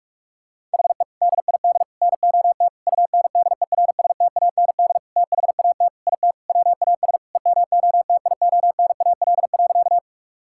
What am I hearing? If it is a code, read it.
Morse code, heard as a keyed tone, about 46 wpm, 700 Hz. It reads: HE DID NOT UNDERSTAND THAT IT WAS EMOTIONAL.
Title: Sentences from Top 1000 Words - Review Part 5 - 45wpm